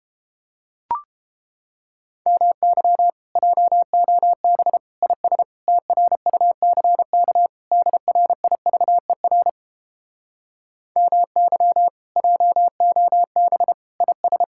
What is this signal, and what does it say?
Morse code, heard as a keyed tone, about 31 wpm, 700 Hz. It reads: MY JOB IS TRUCK DRIVER MY JOB IS